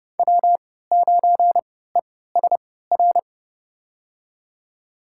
Morse code, heard as a keyed tone, about 30 wpm, spaced as for 17 wpm, 700 Hz. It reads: W9ESR
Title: USA Call Signs - 3x Character Spacing - 30wpm